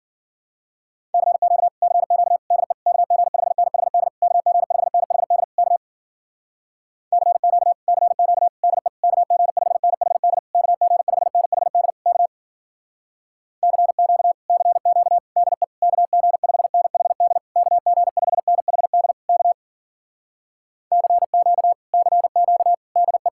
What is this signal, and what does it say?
Morse code, heard as a keyed tone, about 53 wpm, 700 Hz. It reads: CQ CQ DE KG5NHD KG5NHD K CQ CQ DE KG5NHD KG5NHD K CQ CQ DE KG5NHD KG5NHD K CQ CQ DE